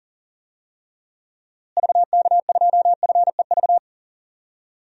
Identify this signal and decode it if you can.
Morse code, heard as a keyed tone, about 37 wpm, 700 Hz. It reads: VK2FEV
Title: DXCC-100 Call Signs - 6 Characters - Repeated 2x - 40wpm